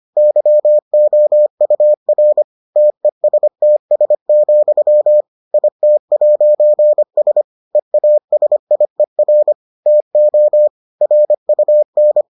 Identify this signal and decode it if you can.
Morse code, heard as a keyed tone, about 25 wpm, 600 Hz. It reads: YOUR TESTS, IT'S EASIER TO RUN